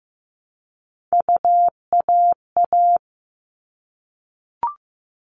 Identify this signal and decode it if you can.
Morse code, heard as a keyed tone, about 15 wpm, 700 Hz. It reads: UAA
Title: All Cut Numbers - 3-Digit 15wpm